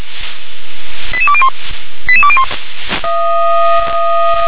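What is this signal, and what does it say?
Five-tone selective call (ZVEI), heard as quick runs of five tones, tones of about 70 ms. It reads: 88201 88201
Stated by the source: ZVEI five-tone call received by SDR.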